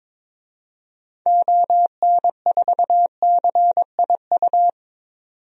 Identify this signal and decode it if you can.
Morse code, heard as a keyed tone, about 22 wpm, 700 Hz. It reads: ON4CIU
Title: DXCC-100 Call Signs - 6 Characters - 22wpm